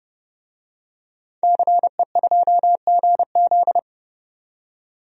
Morse code, heard as a keyed tone, about 31 wpm, 700 Hz. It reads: CE2GZ
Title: DXCC-100 Call Signs - 5 Characters - 30wpm